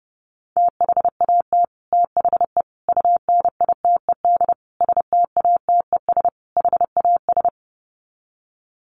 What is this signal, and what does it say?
Morse code, heard as a keyed tone, about 30 wpm, 700 Hz. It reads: THAT THE UNITED STATES HAS
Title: Sets of 5 Words Encore 30wpm